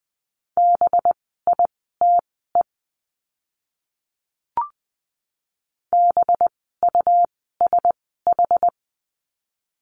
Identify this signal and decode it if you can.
Morse code, heard as a keyed tone, about 20 wpm, spaced as for 15 wpm, 700 Hz. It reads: BITE BUSH
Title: 185 Lesson 30f - Letter b - Words 20wpm